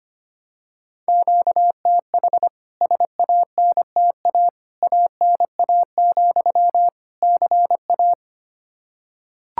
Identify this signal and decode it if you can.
Morse code, heard as a keyed tone, about 25 wpm, 700 Hz. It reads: QTH SANTA ANA, CA
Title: K1USN SST QSO Practice 25wpm